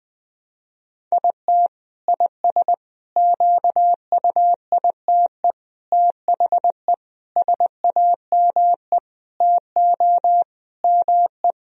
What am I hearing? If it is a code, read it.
Morse code, heard as a keyed tone, about 20 wpm, 700 Hz. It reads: IT IS QUITE THE SAME TO ME
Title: Sentences from Top 1000 Words - Review Part 3 - 20wpm